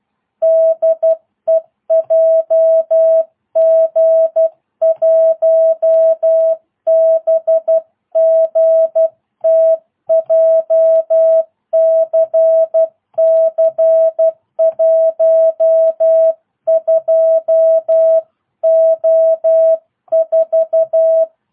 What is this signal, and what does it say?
Morse code, heard as a keyed tone, about 12 wpm, 660 Hz. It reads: DE JG1BGT JCC 12O4